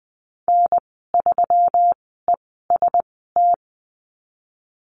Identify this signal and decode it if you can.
Morse code, heard as a keyed tone, about 20 wpm, spaced as for 15 wpm, 700 Hz. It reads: N3EST
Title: 147 Lesson 24g - Letter y - Call Signs 20wpm